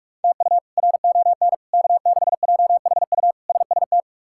Morse code, heard as a keyed tone, about 42 wpm, 700 Hz. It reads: TU RON K6JHU SST
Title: K1USN SST QSO Practice - 45wpm